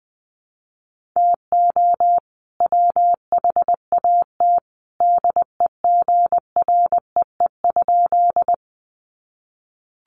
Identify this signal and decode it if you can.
Morse code, heard as a keyed tone, about 20 wpm, 700 Hz. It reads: TO WHAT DEGREE?
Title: Sentences from Top 1500 Words 20wpm